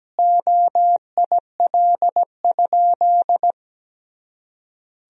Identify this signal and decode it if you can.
Morse code, heard as a keyed tone, about 17 wpm, 700 Hz. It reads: OIL?
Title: Sentences from Top 1000 Words - Review Part 4 - 17wpm